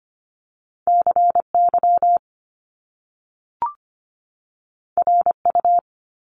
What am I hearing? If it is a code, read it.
Morse code, heard as a keyed tone, about 25 wpm, 700 Hz. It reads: C Y R U
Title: Top Two Letter Combinations in Words 25wpm